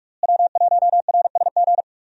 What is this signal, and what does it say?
Morse code, heard as a keyed tone, about 46 wpm, 700 Hz. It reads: W1RSG
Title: USA Call Signs - 5 Characters - Repeated 3x - 45wpm